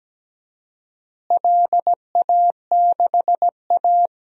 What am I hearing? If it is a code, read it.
Morse code, heard as a keyed tone, about 17 wpm, 700 Hz. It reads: LA6A
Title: DXCC-100 Call Signs - 4 Characters - Repeated 3x - 17wpm